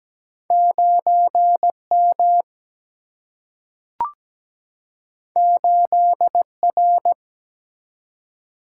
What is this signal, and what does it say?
Morse code, heard as a keyed tone, about 17 wpm, 700 Hz. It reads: 9M 8R
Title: Single-Letter Single-Number 17wpm